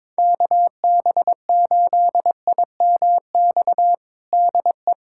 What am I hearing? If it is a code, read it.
Morse code, heard as a keyed tone, about 22 wpm, 700 Hz. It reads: KB8IMX DE
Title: USA Callsign DE Callsign 22wpm